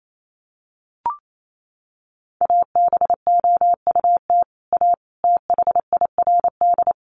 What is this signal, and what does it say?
Morse code, heard as a keyed tone, about 28 wpm, 700 Hz. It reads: ABOUT A THIRD
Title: Sets of 5 Words Encore2 28wpm